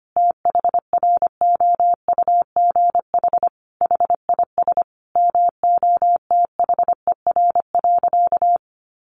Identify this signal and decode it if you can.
Morse code, heard as a keyed tone, about 25 wpm, 700 Hz. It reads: THROUGH HIS MOTHER.
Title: Sentences from Top 1200 Words 25wpm